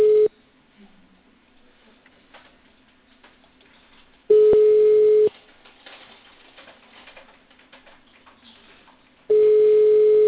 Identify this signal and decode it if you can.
Ringback tone (Europe, 425 Hz).